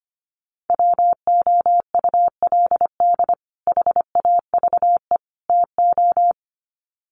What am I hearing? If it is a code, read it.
Morse code, heard as a keyed tone, about 25 wpm, 700 Hz. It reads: WOULD HAVE TO
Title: Sets of 3 Words 25wpm